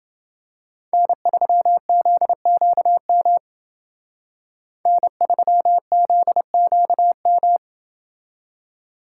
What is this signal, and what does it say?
Morse code, heard as a keyed tone, about 28 wpm, 700 Hz. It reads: N3ZQM N3ZQM